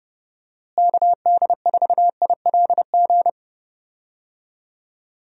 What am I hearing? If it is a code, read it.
Morse code, heard as a keyed tone, about 31 wpm, 700 Hz. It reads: KD4ILG
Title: USA Call Signs - 6 Characters - Repeated 2x - 30wpm